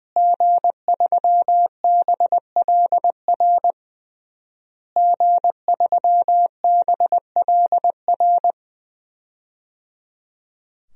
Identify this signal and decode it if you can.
Morse code, heard as a keyed tone, about 20 wpm, 700 Hz. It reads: G3BLR G3BLR
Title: DXCC-100 Call Signs - 5 Characters - Repeated 3x - 20wpm